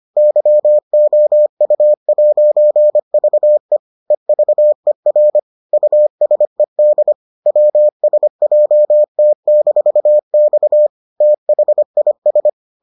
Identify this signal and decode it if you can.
Morse code, heard as a keyed tone, about 25 wpm, 600 Hz. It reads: YOU'VE EVER USED WSJT-X THIS